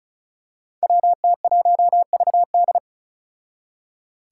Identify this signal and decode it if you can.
Morse code, heard as a keyed tone, about 36 wpm, 700 Hz. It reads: WT1VD
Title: Five Challenging Letter-Number Combinations 35wpm